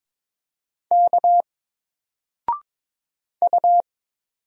Morse code, heard as a keyed tone, about 22 wpm, 700 Hz. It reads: K U
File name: Single-Letter-Number-22wpm